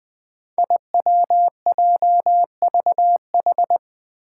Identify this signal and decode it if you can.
Morse code, heard as a keyed tone, about 20 wpm, 700 Hz. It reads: IWJVH
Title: Five Challenging Letter-Number Combinations 20wpm